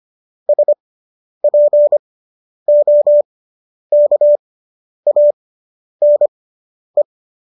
Morse code, heard as a keyed tone, about 25 wpm, spaced as for 10 wpm, 590 Hz. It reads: SPOKANE